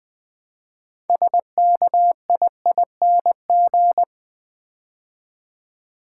Morse code, heard as a keyed tone, about 20 wpm, 700 Hz. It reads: SKIING
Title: Build A Word - 3 Letters At A Time - 6-Letter Words - 20wpm